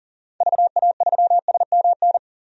Morse code, heard as a keyed tone, about 40 wpm, 700 Hz. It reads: VA3SMN
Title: DXCC-100 Call Signs - 6 Characters - Repeated 2x - 40wpm